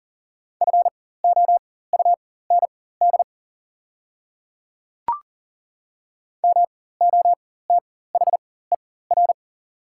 Morse code, heard as a keyed tone, about 41 wpm, spaced as for 19 wpm, 700 Hz. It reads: FOUND MOTHER
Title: 4x Character Spacing - Top 300 Words - 40wpm